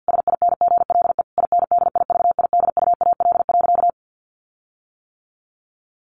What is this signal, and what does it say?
Morse code, heard as a keyed tone, about 48 wpm, 700 Hz. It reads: SINGLE INDIVIDUAL.
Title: Sentences from Top 1500 Words 50wpm